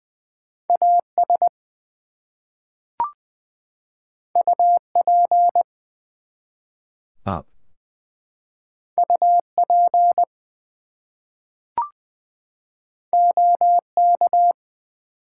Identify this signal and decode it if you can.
Morse code, heard as a keyed tone, about 20 wpm, 700 Hz. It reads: AS UP UP OK
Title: Two Letter Words - 20wpm